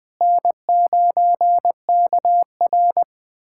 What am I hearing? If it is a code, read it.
Morse code, heard as a keyed tone, about 20 wpm, 700 Hz. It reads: N9KR
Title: K1USN SST QSO Practice - 20wpm